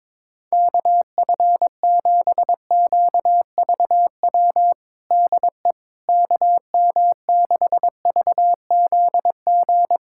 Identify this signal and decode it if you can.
Morse code, heard as a keyed tone, about 21 wpm, 700 Hz. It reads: KF7QVW DE KM6VZG
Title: USA Callsign DE Callsign 22wpm